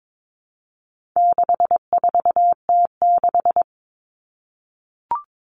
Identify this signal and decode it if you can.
Morse code, heard as a keyed tone, about 22 wpm, 700 Hz. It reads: 64T6
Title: Common Cut Numbers - 4-Digit 22wpm